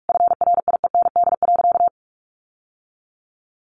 Morse code, heard as a keyed tone, about 44 wpm, 700 Hz. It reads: FRIEND.
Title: Sentences from Top 1300 Words 45wpm